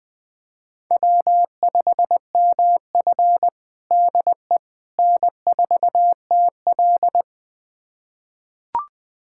Morse code, heard as a keyed tone, about 21 wpm, 700 Hz. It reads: W5MF DE N4TL
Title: USA Callsign DE Callsign - 4 Characters - 20wpm